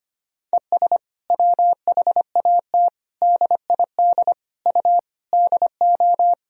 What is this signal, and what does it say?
Morse code, heard as a keyed tone, about 25 wpm, 700 Hz. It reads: ES WHAT DID U DO
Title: Sentences from Top 300 Words and Abbreviations 25wpm